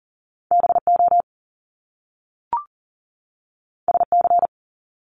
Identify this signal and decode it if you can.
Morse code, heard as a keyed tone, about 40 wpm, 700 Hz. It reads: BO SC
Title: Top Two Letter Combinations in Words 40wpm